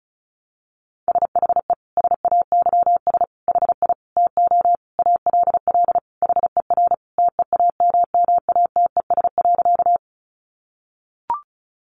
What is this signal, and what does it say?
Morse code, heard as a keyed tone, about 34 wpm, 700 Hz. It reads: SHE SAYS HI TO ALL HER TEAMMATES.
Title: Easy Phrases 35wpm